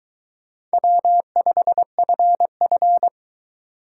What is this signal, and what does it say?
Morse code, heard as a keyed tone, about 23 wpm, 700 Hz. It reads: W5FF